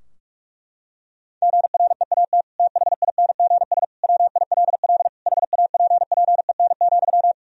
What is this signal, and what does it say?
Morse code, heard as a keyed tone, about 41 wpm, 700 Hz. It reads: GREAT THINGS WILL HAPPEN,
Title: Sentences from Top 1000 Words - Review Part 4 - 45wpm